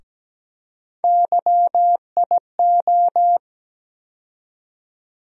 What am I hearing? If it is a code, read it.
Morse code, heard as a keyed tone, about 17 wpm, 700 Hz. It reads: YIO
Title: USA Call Sign Postfixes - 3 Letters 17wpm